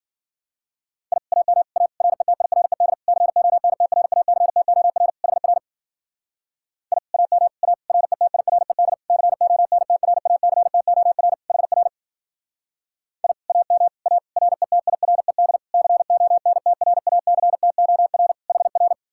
Morse code, heard as a keyed tone, about 48 wpm, 700 Hz. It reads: I AM A RETIRED CONTRACTOR HR I AM A RETIRED CONTRACTOR HR I AM A RETIRED CONTRACTOR HR
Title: Speed-Racing Common QSO Phrases - 40wpm